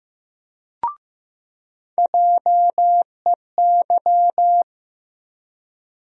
Morse code, heard as a keyed tone, about 15 wpm, 700 Hz. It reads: JEY
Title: USA Call Sign Postfixes - 3 Letters 15wpm